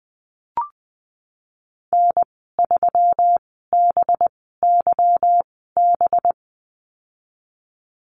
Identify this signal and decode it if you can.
Morse code, heard as a keyed tone, about 19 wpm, spaced as for 14 wpm, 700 Hz. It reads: N3BYB
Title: 245 Lesson 39g - Letter z - Call Signs 20wpm